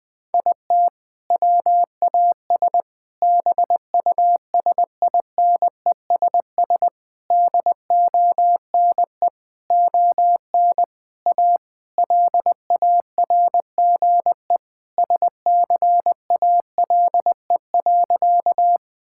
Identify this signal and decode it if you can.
Morse code, heard as a keyed tone, about 20 wpm, 700 Hz. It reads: IT WAS BUSINESS DONE ON A LARGE SCALE.
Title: Sentences from Top 1000 Words - Review Part 5 - 20wpm